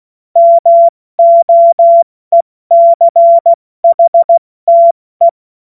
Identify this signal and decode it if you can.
Morse code, heard as a keyed tone, about 16 wpm, 680 Hz. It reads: MOECHTE
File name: Morsekurs / Witze / Witz 17. August 2021, 70 F80